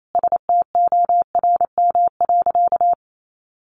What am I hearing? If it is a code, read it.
Morse code, heard as a keyed tone, about 28 wpm, 700 Hz. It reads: STORM.
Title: Sentences from Top 1500 Words 28wpm